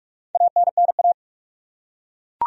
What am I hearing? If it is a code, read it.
Morse code, heard as a keyed tone, about 41 wpm, 700 Hz. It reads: ANNA
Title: USA Common Names - Max 5 Letters - 45wpm